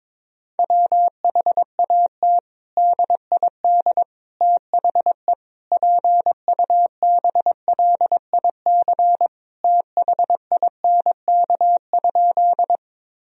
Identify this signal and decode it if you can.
Morse code, heard as a keyed tone, about 22 wpm, 700 Hz. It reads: WHAT DID THE PUBLIC THINK?
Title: Sentences from Top 1000 Words - Review Part 3 - 22wpm